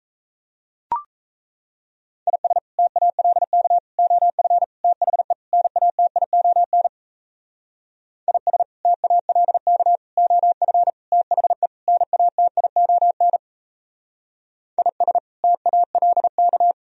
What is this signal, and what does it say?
Morse code, heard as a keyed tone, about 35 wpm, 700 Hz. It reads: IS TALK OF THE NATION IS TALK OF THE NATION IS TALK